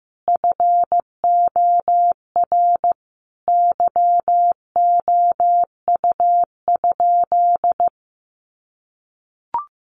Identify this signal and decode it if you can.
Morse code, heard as a keyed tone, about 15 wpm, 700 Hz. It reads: FOR YOU?
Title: Sentences from Top 1400 Words 15wpm